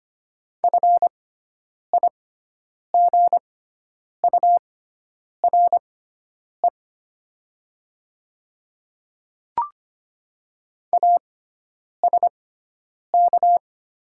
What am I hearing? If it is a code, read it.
Morse code, heard as a keyed tone, about 25 wpm, spaced as for 9 wpm, 700 Hz. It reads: FIGURE ASK
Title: 6x Character Spacing - Top 500 Words - 25wpm